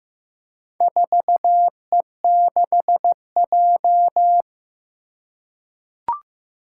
Morse code, heard as a keyed tone, about 15 wpm, 700 Hz. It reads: 4E6J
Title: Four Challenging Letter-Number Combinations 15wpm